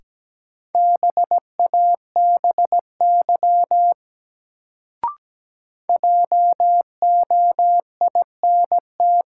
Morse code, heard as a keyed tone, about 17 wpm, 700 Hz. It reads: BABY JOINT
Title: Sentences from Top 1300 Words - New Word Drill Alt - 17wpm